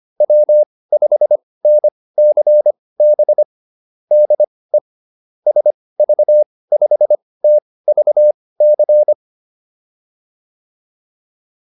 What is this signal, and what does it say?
Morse code, heard as a keyed tone, about 25 wpm, spaced as for 18 wpm, 600 Hz. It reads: W5NCB DE SV5TVC
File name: QSO408 18